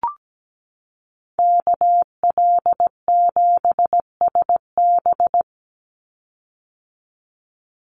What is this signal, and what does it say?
Morse code, heard as a keyed tone, about 17 wpm, 700 Hz. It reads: KL7SB
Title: Call Signs - 17wpm